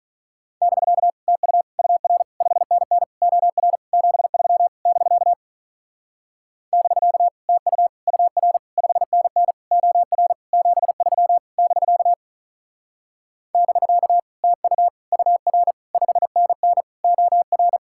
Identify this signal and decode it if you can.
Morse code, heard as a keyed tone, about 44 wpm, 700 Hz. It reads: <BK> TU UR 5NN OR 73 <BK> <BK> TU UR 5NN OR 73 <BK> <BK> TU UR 5NN OR